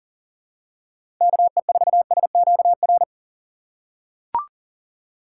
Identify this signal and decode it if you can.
Morse code, heard as a keyed tone, about 40 wpm, 700 Hz. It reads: KE4SQR
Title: USA Call Signs - 6 Characters - 6x Character Spacing - 40wpm